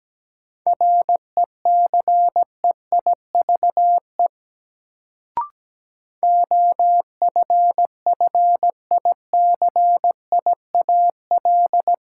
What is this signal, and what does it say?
Morse code, heard as a keyed tone, about 17 wpm, 700 Hz. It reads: RECEIVE OFFICIAL
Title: Sentences from Top 1000 Words - New Word Drill - 17wpm